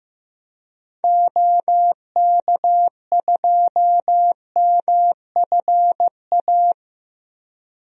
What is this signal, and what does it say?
Morse code, heard as a keyed tone, about 16 wpm, 700 Hz. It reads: OK2MFA
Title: DXCC-100 Call Signs - 6 Characters - Repeated 3x - 15wpm